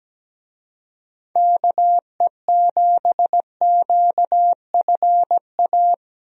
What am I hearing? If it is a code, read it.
Morse code, heard as a keyed tone, about 17 wpm, 700 Hz. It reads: KE7QFA
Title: USA Call Signs - 6 Characters - Repeated 2x - 17wpm